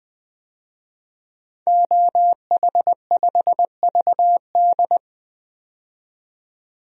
Morse code, heard as a keyed tone, about 20 wpm, 700 Hz. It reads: OH5VD
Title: DXCC-100 Call Signs - 5 Characters - 20wpm